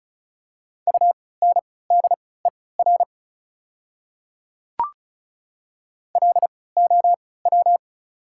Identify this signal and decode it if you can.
Morse code, heard as a keyed tone, about 36 wpm, spaced as for 20 wpm, 700 Hz. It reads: UNDER LOW
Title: 3x Character Spacing - Top 200 Words - 35wpm